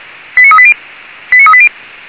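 Five-tone selective call (ZVEI), heard as quick runs of five tones, tones of about 70 ms. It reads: 89380 89380
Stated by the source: Five-tone ZVEI selective call received on an SDR.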